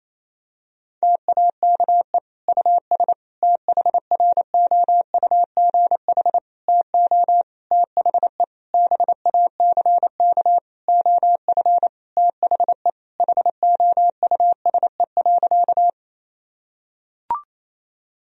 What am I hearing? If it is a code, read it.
Morse code, heard as a keyed tone, about 27 wpm, 700 Hz. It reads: TAKE US THROUGH TO THE BACK OF THE HOUSE.
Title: Sentences from Top 1000 Words - Review Part 1 - 28wpm